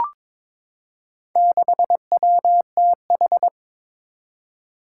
Morse code, heard as a keyed tone, about 21 wpm, 700 Hz. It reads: 6WTH
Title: Four Challenging Letter-Number Combinations 22wpm